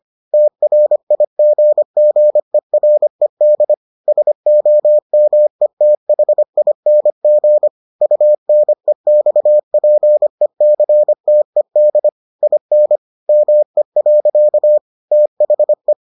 Morse code, heard as a keyed tone, about 25 wpm, 600 Hz. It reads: TRIGGERED SOMETHING UNEXPECTED IN ME. THE